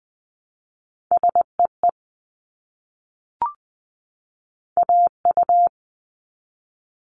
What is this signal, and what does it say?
Morse code, heard as a keyed tone, about 20 wpm, 700 Hz. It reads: SEE AU
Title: Multi-letter Phonemes 20wpm